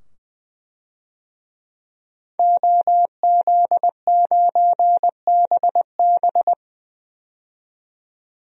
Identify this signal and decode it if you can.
Morse code, heard as a keyed tone, about 20 wpm, 700 Hz. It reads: OZ9BB
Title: DXCC-100 Call Signs - 5 Characters - 20wpm